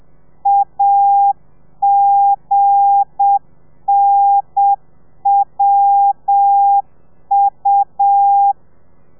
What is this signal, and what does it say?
Morse code, heard as a keyed tone, about 7 wpm, 800 Hz. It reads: AGNWU
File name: telegrafia / i2tfj / Lez8 35